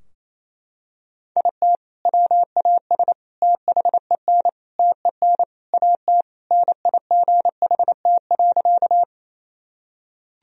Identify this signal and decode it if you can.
Morse code, heard as a keyed tone, about 28 wpm, 700 Hz. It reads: IT WAS THEN TEN AT NIGHT.
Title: Sentences from Top 1000 Words - Review Part 2 - 28wpm